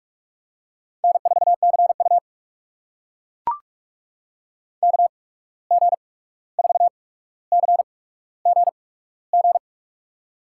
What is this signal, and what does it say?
Morse code, heard as a keyed tone, about 44 wpm, spaced as for 17 wpm, 700 Hz. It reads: N4CU KG4CGG
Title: USA Call Signs - 8x Character Spacing - 45wpm